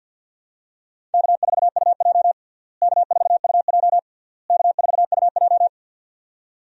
Morse code, heard as a keyed tone, about 47 wpm, 700 Hz. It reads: K4UJ K4UJ K4UJ
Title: USA Call Signs - 4 Characters - Repeated 3x - 50wpm